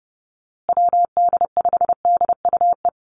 Morse code, heard as a keyed tone, about 30 wpm, 700 Hz. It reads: WD5DUE
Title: USA Call Signs Repeated 3x 30wpm